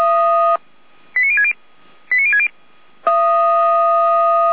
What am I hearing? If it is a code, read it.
Five-tone selective call (ZVEI), heard as quick runs of five tones, tones of about 70 ms. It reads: 89060 89060